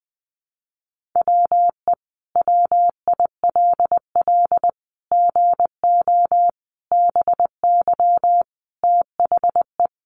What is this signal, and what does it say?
Morse code, heard as a keyed tone, about 20 wpm, 700 Hz. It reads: WE WILL GO BY THE
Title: Sentences from Top 1500 Words 20wpm